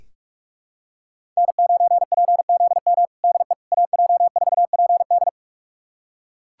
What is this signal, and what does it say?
Morse code, heard as a keyed tone, about 41 wpm, 700 Hz. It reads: N9PZM DE AJ4PD
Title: USA Callsign DE Callsign - 5 Characters - 45wpm